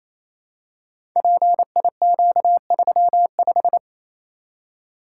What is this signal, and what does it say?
Morse code, heard as a keyed tone, about 27 wpm, 700 Hz. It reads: PIQ35
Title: Code Groups - 5 Letter:Numbers - 28wpm